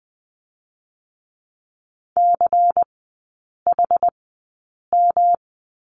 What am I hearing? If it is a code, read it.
Morse code, heard as a keyed tone, about 20 wpm, spaced as for 8 wpm, 700 Hz. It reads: CHM
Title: 121 Lesson 20e - Number 3 - Three Characters 20wpm